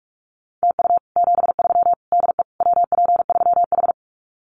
Morse code, heard as a keyed tone, about 44 wpm, 700 Hz. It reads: TU 73 DE WP3H
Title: Common QSO Phrases Encore 45wpm